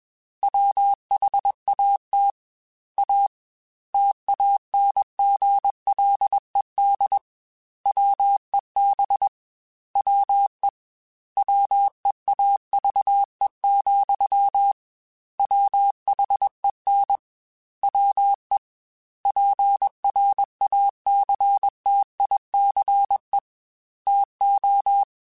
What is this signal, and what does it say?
Morse code, heard as a keyed tone, about 21 wpm, 800 Hz. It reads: WHAT A TANGLED WEB WE WEAVE, WHEN WE PRACTICE TO
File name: Quotes / Quotes-mp3s / Deceive @20